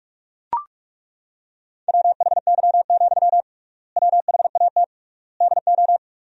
Morse code, heard as a keyed tone, about 42 wpm, 700 Hz. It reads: WHY, WHAT DO
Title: 3x Word-Spacing - Sentences from Top 200 Words - 45wpm